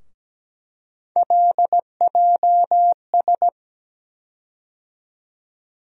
Morse code, heard as a keyed tone, about 17 wpm, 700 Hz. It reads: LJS
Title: USA Call Sign Postfixes - 3 Letters 17wpm